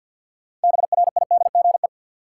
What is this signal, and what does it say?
Morse code, heard as a keyed tone, about 53 wpm, 700 Hz. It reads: BRIDGE
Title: Sentences from Top 1200 Words - New Word Drill - 50wpm